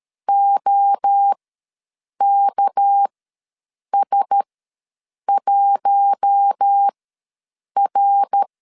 Morse code, heard as a keyed tone, about 13 wpm, spaced as for 7 wpm, 800 Hz. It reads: OKS1R
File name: LEZIONI / 6 PKX0 / 6 PK ws 12wpm 3116